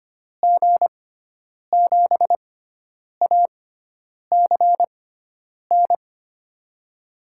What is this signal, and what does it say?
Morse code, heard as a keyed tone, about 25 wpm, spaced as for 9 wpm, 700 Hz. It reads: G7ACN
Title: USA Call Signs - 6 Characters - 6x Character Spacing - 25wpm